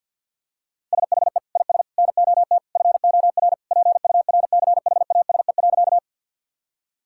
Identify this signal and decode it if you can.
Morse code, heard as a keyed tone, about 46 wpm, 700 Hz. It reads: SHE IS NOT FOR PURCHASE.
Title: Sentences from Top 1000 Words - Review Part 5 - 50wpm